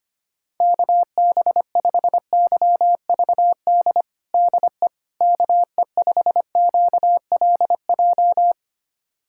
Morse code, heard as a keyed tone, about 25 wpm, 700 Hz. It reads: KB5YVD DE KE5QLJ
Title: USA Callsign DE Callsign - 6 Characters - 25wpm